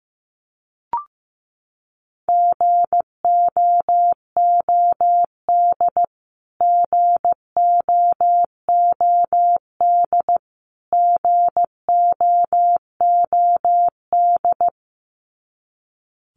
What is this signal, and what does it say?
Morse code, heard as a keyed tone, about 15 wpm, 700 Hz. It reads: GOOD GOOD GOOD
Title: Top 300 Words Repeated 15wpm